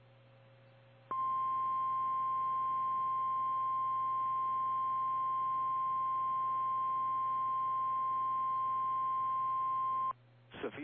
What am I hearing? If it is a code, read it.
Emergency Alert System (SAME) tones (the 1050 Hz attention tone). It is heard from 1.1 to 10.2 s.
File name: eas / KKPRKXPN / KKPRKXPN